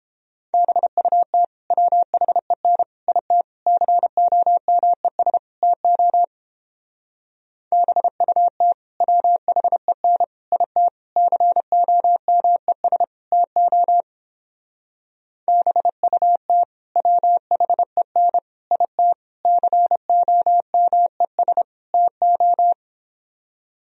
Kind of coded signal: Morse code

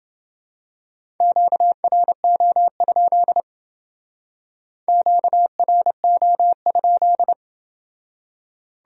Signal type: Morse code

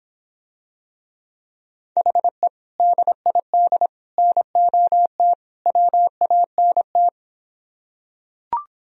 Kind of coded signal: Morse code

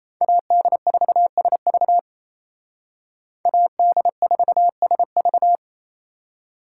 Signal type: Morse code

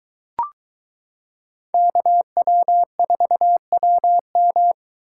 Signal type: Morse code

